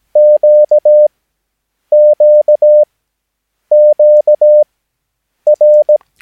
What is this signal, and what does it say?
Morse code, heard as a keyed tone, about 17 wpm, spaced as for 8 wpm, 600 Hz. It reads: QQQR
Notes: Övningsmaterial med alfabetet A till Ö samt siffror 0 till 9 enligt International/Continental Morse Code.